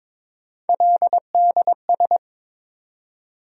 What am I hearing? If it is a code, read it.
Morse code, heard as a keyed tone, about 22 wpm, 700 Hz. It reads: LDS
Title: USA Call Sign Postfixes - 3 Letters 22wpm